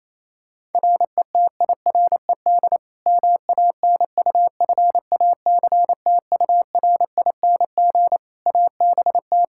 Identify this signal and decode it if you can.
Morse code, heard as a keyed tone, about 28 wpm, 700 Hz. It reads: RETIRED MANUFACTURING ABT
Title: Practice QSO - Exchange 3 - More Info - 28wpm